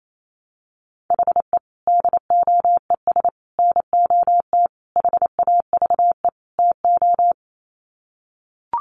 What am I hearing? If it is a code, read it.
Morse code, heard as a keyed tone, about 28 wpm, 700 Hz. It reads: HE DOES NOT HAVE TO
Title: Sets of 5 Words Encore 28wpm